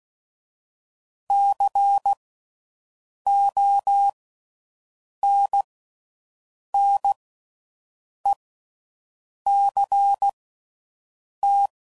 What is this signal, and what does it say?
Morse code, heard as a keyed tone, about 16 wpm, spaced as for 6 wpm, 800 Hz. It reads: CONNECT
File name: Slow With Faster Ings